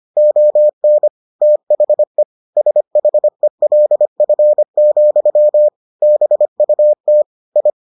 Morse code, heard as a keyed tone, about 25 wpm, 600 Hz. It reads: ON THE SHELF, BUT I